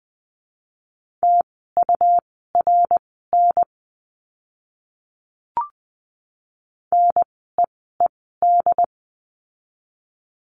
Morse code, heard as a keyed tone, about 20 wpm, spaced as for 15 wpm, 700 Hz. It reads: TURN NEED
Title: 101 Lesson 17f - Letter u - Words 20wpm